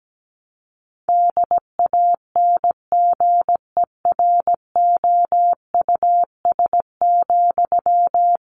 Morse code, heard as a keyed tone, about 17 wpm, 700 Hz. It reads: DANGEROUS,
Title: Sentences from Top 1200 Words 17wpm